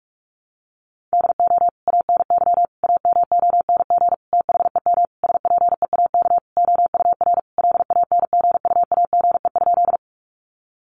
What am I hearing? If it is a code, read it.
Morse code, heard as a keyed tone, about 46 wpm, 700 Hz. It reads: DO ANY AMONG THEM SPEAK OUR LANGUAGE?
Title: Sentences from Top 1100 Words 45wpm